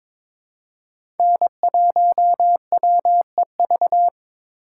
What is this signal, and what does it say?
Morse code, heard as a keyed tone, about 23 wpm, 700 Hz. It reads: N1WEV
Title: USA Call Signs - 5 Characters - 6x Character Spacing - 22wpm